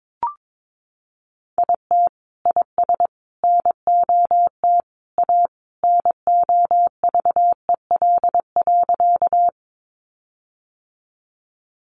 Morse code, heard as a keyed tone, about 22 wpm, 700 Hz. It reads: IT IS NOT A NOVEL.
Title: Sentences from Top 1300 Words 22wpm